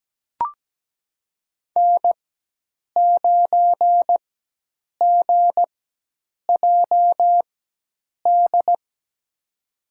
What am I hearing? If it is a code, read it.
Morse code, heard as a keyed tone, about 17 wpm, spaced as for 8 wpm, 700 Hz. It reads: N9GJD
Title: USA Call Signs - 5 Characters - 4x Character Spacing - 17wpm